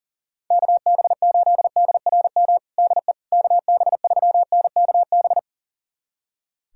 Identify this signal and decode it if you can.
Morse code, heard as a keyed tone, about 40 wpm, 700 Hz. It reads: KB8DRM DE KB3NKB
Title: USA Callsign DE Callsign - 6 Characters - 40wpm